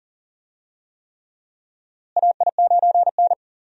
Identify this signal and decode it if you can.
Morse code, heard as a keyed tone, about 40 wpm, 700 Hz. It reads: AI9N
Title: USA Call Signs - 4 Characters - Repeated 3x - 40wpm